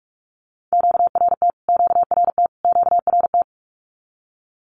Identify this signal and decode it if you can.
Morse code, heard as a keyed tone, about 44 wpm, 700 Hz. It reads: QRT QRT QRT
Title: Q-Codes Repeated 3x 45wpm